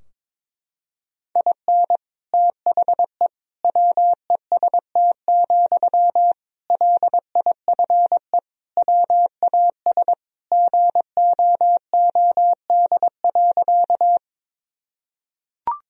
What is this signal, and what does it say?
Morse code, heard as a keyed tone, about 22 wpm, 700 Hz. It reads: IN THE WEST, LIFE WAS GOOD.
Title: Sentences from Top 1000 Words - Review Part 2 - 22wpm